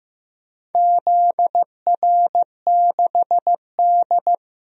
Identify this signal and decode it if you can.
Morse code, heard as a keyed tone, about 15 wpm, 700 Hz. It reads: ZR6D
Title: DXCC-100 Call Signs - 4 Characters - 15wpm